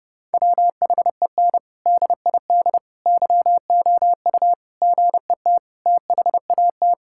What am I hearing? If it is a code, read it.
Morse code, heard as a keyed tone, about 32 wpm, 700 Hz. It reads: WHEN DID YOU GET THAT
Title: Sentences from Top 1000 Words - Review Part 6 - 30wpm